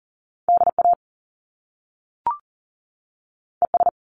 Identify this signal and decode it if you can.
Morse code, heard as a keyed tone, about 40 wpm, 700 Hz. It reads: DA ES
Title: Top Two Letter Combinations in Words 40wpm